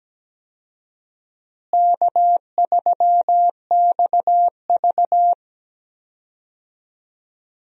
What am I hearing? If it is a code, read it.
Morse code, heard as a keyed tone, about 17 wpm, 700 Hz. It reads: K3XV
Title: USA Call Signs - 4 Characters - Repeated 2x - 17wpm